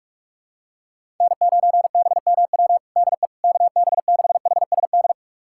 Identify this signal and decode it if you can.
Morse code, heard as a keyed tone, about 46 wpm, 700 Hz. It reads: N9BMW DE KB6HSD
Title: USA Callsign DE Callsign 45wpm